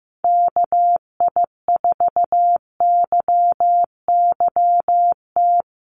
Morse code, heard as a keyed tone, about 15 wpm, 700 Hz. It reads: KI4YYT
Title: USA Call Signs 15wpm